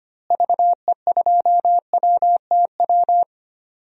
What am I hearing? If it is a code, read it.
Morse code, heard as a keyed tone, about 25 wpm, 700 Hz. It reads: VE2WTW
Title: DXCC-100 Call Signs - 6 Characters - Repeated 2x - 25wpm